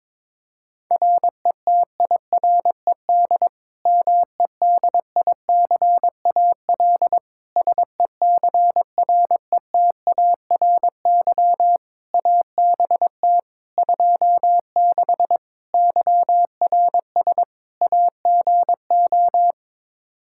Morse code, heard as a keyed tone, about 21 wpm, 700 Hz. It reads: RETIRED MEDICAL SECRETARY ABT 26 YRS AGO
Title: K1USN SST QSO Practice 22wpm